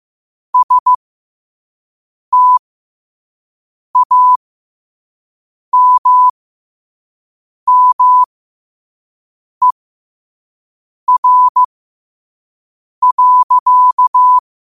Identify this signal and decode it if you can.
Morse code, heard as a keyed tone, about 15 wpm, spaced as for 5 wpm, 1000 Hz. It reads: STAMMER.
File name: qotdfiles / 5wpm